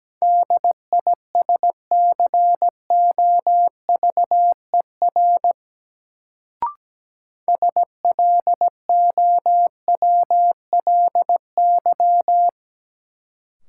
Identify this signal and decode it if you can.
Morse code, heard as a keyed tone, about 17 wpm, 700 Hz. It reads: DISCOVER SLOWLY
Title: Sentences from Top 1100 Words - New Word Drill Alt - 17wpm